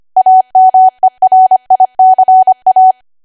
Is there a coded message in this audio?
Yes: Morse code from 0.2 to 2.9 s, heard as a keyed tone, about 25 wpm, 750 Hz. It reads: AMERICA